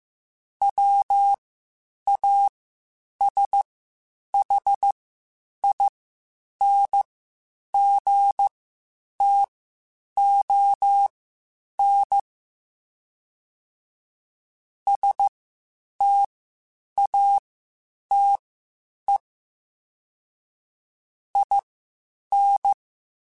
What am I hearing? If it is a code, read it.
Morse code, heard as a keyed tone, about 15 wpm, spaced as for 8 wpm, 800 Hz. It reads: WASHINGTON STATE IN
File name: (spaced) / QSOs Spaced-mp3s / WD0BGP 1 @13 sp